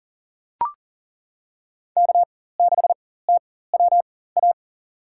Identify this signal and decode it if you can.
Morse code, heard as a keyed tone, about 40 wpm, spaced as for 19 wpm, 700 Hz. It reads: K6TWA
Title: USA Call Signs - 5 Characters - 4x Character Spacing - 40wpm